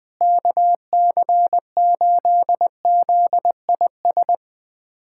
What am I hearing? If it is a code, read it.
Morse code, heard as a keyed tone, about 21 wpm, 700 Hz. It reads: KC8ZIS
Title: USA Callsign DE Callsign 20wpm